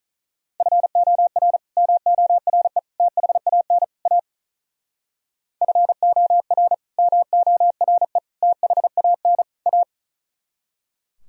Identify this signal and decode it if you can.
Morse code, heard as a keyed tone, about 45 wpm, 700 Hz. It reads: FOR MORE THAN A FOR MORE THAN A